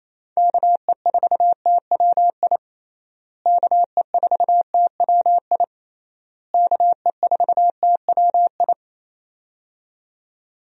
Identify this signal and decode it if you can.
Morse code, heard as a keyed tone, about 28 wpm, 700 Hz. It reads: KE4TWI KE4TWI KE4TWI
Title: USA Call Signs - 6 Characters - Repeated 3x - 28wpm